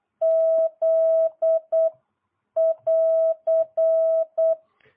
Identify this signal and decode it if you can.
Morse code, heard as a keyed tone, about 8 wpm, 650 Hz. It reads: Z <AR>